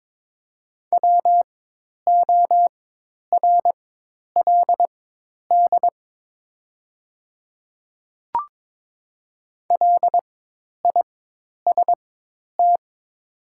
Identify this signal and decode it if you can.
Morse code, heard as a keyed tone, about 22 wpm, spaced as for 10 wpm, 700 Hz. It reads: WORLD LIST
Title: 4x Character Spacing - Top 500 Words - 22wpm